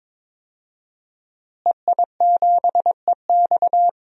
Morse code, heard as a keyed tone, about 22 wpm, 700 Hz. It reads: EI7EX
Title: DXCC-100 Call Signs - 5 Characters - Repeated 3x - 22wpm